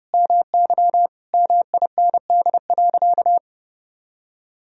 Morse code, heard as a keyed tone, about 30 wpm, 700 Hz. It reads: MY MIND.
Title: Sentences from Top 1000 Words - Review Part 1 - 30wpm